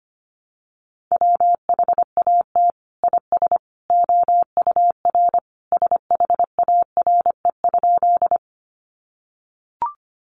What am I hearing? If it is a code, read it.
Morse code, heard as a keyed tone, about 25 wpm, 700 Hz. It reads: WHAT IS OUR SHARE?
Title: Sentences from Top 900 Words 25wpm